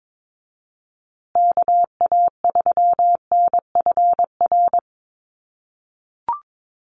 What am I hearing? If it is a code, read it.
Morse code, heard as a keyed tone, about 22 wpm, 700 Hz. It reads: KA3NFR
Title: USA Call Signs Repeated 3x 22wpm